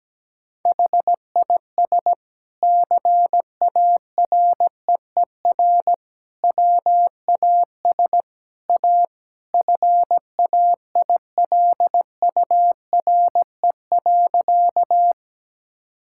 Morse code, heard as a keyed tone, about 17 wpm, 700 Hz. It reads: HIS CAREER WAS A FAILURE.
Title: Sentences from Top 1000 Words - Review Part 6 - 17wpm